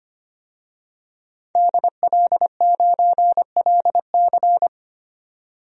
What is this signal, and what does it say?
Morse code, heard as a keyed tone, about 25 wpm, 700 Hz. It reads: DL9LC
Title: DXCC-100 Call Signs - 5 Characters - 25wpm